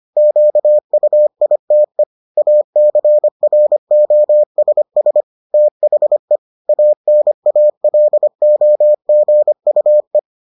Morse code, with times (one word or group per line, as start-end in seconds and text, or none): 0.2-2.0 QUITE
2.4-5.2 ACROSS
5.5-6.4 THE
6.7-10.2 ANALOGUE